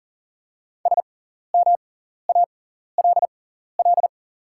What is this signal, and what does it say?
Morse code, heard as a keyed tone, about 40 wpm, spaced as for 14 wpm, 700 Hz. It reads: SMALL
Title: 6x Character Spacing - Top 200 Words - 40wpm